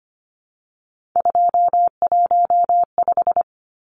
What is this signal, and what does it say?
Morse code, heard as a keyed tone, about 25 wpm, 700 Hz. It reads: 215
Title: Numbers - 3-Digit 25wpm